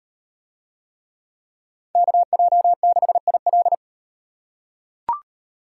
Morse code, heard as a keyed tone, about 38 wpm, 700 Hz. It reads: KJ6IL